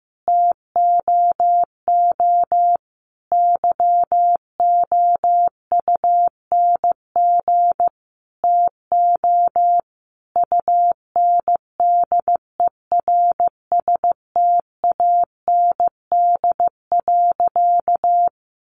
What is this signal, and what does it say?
Morse code, heard as a keyed tone, about 15 wpm, 700 Hz. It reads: TOO YOUNG TO UNDERSTAND.
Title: Sentences from Top 600 Words 15wpm